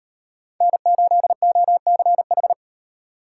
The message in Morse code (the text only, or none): N8OCH